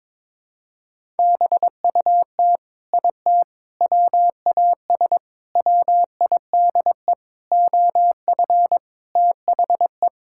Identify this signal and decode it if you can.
Morse code, heard as a keyed tone, about 21 wpm, 700 Hz. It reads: BUT IT WAS WIDE OF THE
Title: Sentences from Top 1000 Words - Review Part 6 - 22wpm